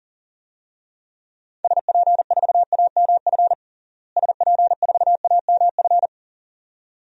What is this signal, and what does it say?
Morse code, heard as a keyed tone, about 40 wpm, 700 Hz. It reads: SP4AMF SP4AMF
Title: DXCC-100 Call Signs - 6 Characters - Repeated 2x - 40wpm